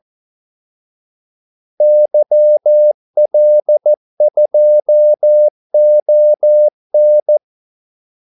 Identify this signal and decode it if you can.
Morse code, heard as a keyed tone, about 14 wpm, 600 Hz. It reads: YL2ON
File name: Callsigns-14wpm